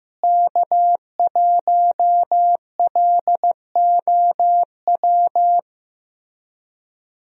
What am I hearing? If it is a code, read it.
Morse code, heard as a keyed tone, about 15 wpm, 700 Hz. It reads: K1LOW
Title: USA Call Signs - 5 Characters - Repeated 3x - 15wpm